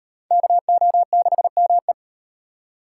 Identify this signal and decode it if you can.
Morse code, heard as a keyed tone, about 37 wpm, 700 Hz. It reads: KO6ME